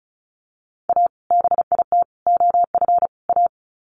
Morse code, heard as a keyed tone, about 34 wpm, 700 Hz. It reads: A BIT OF A
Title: Sets of 4 Words 35wpm